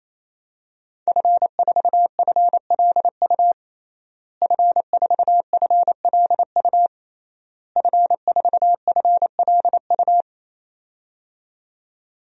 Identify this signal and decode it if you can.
Morse code, heard as a keyed tone, about 28 wpm, 700 Hz. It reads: F4FLU F4FLU F4FLU
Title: DXCC-100 Call Signs - 5 Characters - Repeated 3x - 28wpm